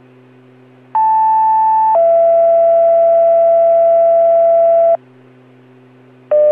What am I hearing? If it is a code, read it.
Two-tone page, heard as a one-second tone, then a three-second tone, about 880 then 670 Hz. It is heard from 0.9 to 5.0 s.